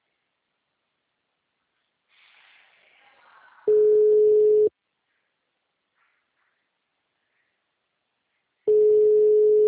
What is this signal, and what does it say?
Ringback tone (Europe, 425 Hz).
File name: out- -2006- - - .9945